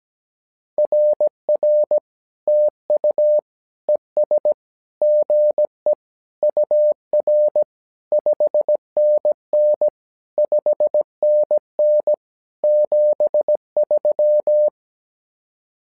Morse code, heard as a keyed tone, about 17 wpm, 600 Hz. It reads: RR TU ES GE UR 5NN 5NN 73
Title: SOTA QSO Practice 17wpm